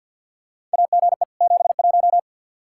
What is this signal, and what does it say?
Morse code, heard as a keyed tone, about 50 wpm, 700 Hz. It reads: AGE 71
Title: Practice QSO - Exchange 3 - More Info - 50wpm